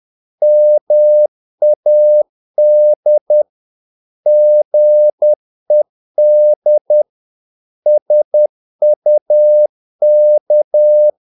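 Morse code, heard as a keyed tone, about 10 wpm, 600 Hz. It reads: MAD GED SUK
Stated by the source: Morse practice of short Danish words at 10 wpm, from CW med Gnister.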